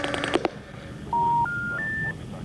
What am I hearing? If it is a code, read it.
Special information tone, with noise.